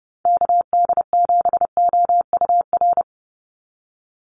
Morse code, heard as a keyed tone, about 30 wpm, 700 Hz. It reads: KD7OUR
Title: USA Call Signs 30wpm